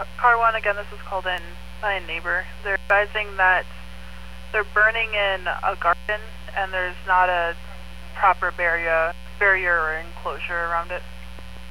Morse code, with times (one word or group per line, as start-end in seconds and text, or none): none